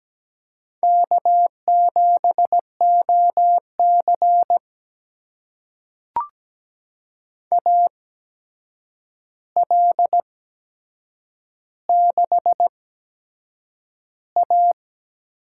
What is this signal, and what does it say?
Morse code, heard as a keyed tone, about 17 wpm, spaced as for 7 wpm, 700 Hz. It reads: K7OC AL6A
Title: USA Call Signs - 4 Characters - 8x Character Spacing - 17wpm